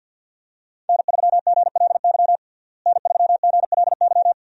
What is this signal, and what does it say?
Morse code, heard as a keyed tone, about 50 wpm, 700 Hz. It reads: N3GLY N3GLY
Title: USA Call Signs - 5 Characters - Repeated 3x - 50wpm